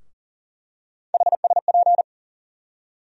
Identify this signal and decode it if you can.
Morse code, heard as a keyed tone, about 38 wpm, 700 Hz. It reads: HSP
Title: USA Call Sign Postfixes - 3 Letters 40wpm